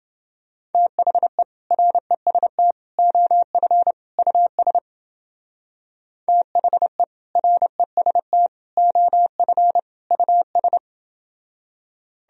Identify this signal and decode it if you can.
Morse code, heard as a keyed tone, about 29 wpm, 700 Hz. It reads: THE REST OF US THE REST OF US